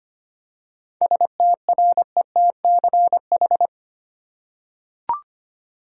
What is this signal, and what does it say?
Morse code, heard as a keyed tone, about 25 wpm, 700 Hz. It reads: STRETCH
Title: Seven Letter Words - 25wpm